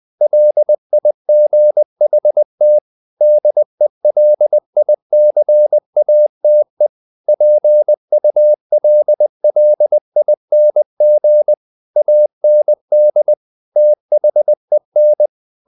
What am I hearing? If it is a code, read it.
Morse code, heard as a keyed tone, about 20 wpm, 600 Hz. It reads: LIGHT DELICATE PULLING AND THEN
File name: the old man and the sea / the old man and the sea part 1 20wpm